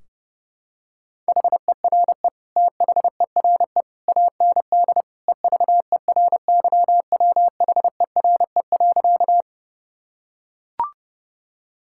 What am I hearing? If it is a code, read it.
Morse code, heard as a keyed tone, about 30 wpm, 700 Hz. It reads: HERE THERE AND EVERYWHERE.
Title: Sentences from Top 1600 Words 30wpm